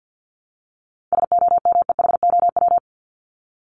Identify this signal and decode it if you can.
Morse code, heard as a keyed tone, about 50 wpm, 700 Hz. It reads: SOMEHOW
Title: Top 1501-1750 Words 50wpm